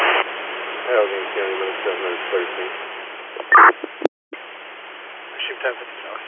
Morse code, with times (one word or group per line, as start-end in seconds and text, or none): none